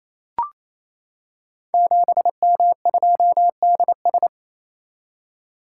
Morse code, heard as a keyed tone, about 27 wpm, 700 Hz. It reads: 7M2DS
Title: Code Groups - 5 Letter:Numbers - 28wpm